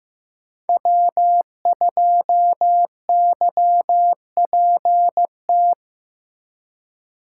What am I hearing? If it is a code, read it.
Morse code, heard as a keyed tone, about 16 wpm, 700 Hz. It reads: W2YPT
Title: USA Call Signs - 5 Characters - 4x Character Spacing - 15wpm